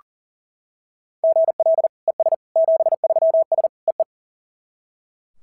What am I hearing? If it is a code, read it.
Morse code, heard as a keyed tone, about 40 wpm, 690 Hz. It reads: GL ES 73S EE
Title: SOTA QSO Practice 40wpm